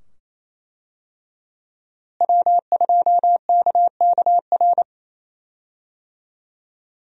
Morse code, heard as a keyed tone, about 27 wpm, 700 Hz. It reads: W2KKR
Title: USA Call Signs - 5 Characters - Repeated 2x - 28wpm